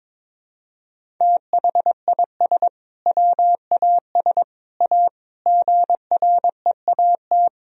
Morse code, heard as a keyed tone, about 22 wpm, 700 Hz. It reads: THIS WAS A GREAT
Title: Sentences from Top 1000 Words - Review Part 1 - 22wpm